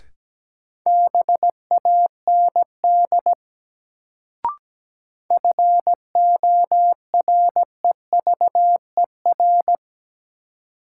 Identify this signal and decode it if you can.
Morse code, heard as a keyed tone, about 17 wpm, 700 Hz. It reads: BAND FOREVER
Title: Sentences from Top 1600 Words - New Word Drill - 17wpm